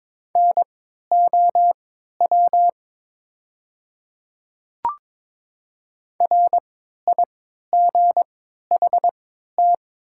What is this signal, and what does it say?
Morse code, heard as a keyed tone, about 22 wpm, spaced as for 12 wpm, 700 Hz. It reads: NOW RIGHT
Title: 3x Character Spacing - Top 200 Words - 22wpm